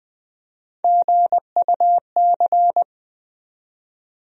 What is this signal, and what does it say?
Morse code, heard as a keyed tone, about 20 wpm, 700 Hz. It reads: GUC
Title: USA Call Sign Postfixes - 3 Letters 20wpm